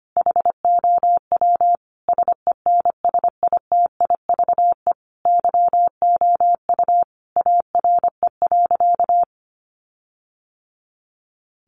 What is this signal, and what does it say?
Morse code, heard as a keyed tone, about 25 wpm, 700 Hz. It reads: HOW SENSITIVE YOU ARE.
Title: Sentences from Top 1400 Words 25wpm